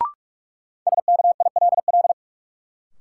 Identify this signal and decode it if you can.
Morse code, heard as a keyed tone, about 47 wpm, 700 Hz. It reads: SKILL
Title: Sentences from Top 1200 Words - New Word Drill - 45wpm